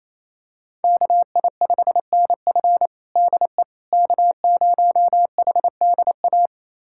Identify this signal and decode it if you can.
Morse code, heard as a keyed tone, about 28 wpm, 700 Hz. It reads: KI5NF DE K0HDA
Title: USA Callsign DE Callsign - 5 Characters - 28wpm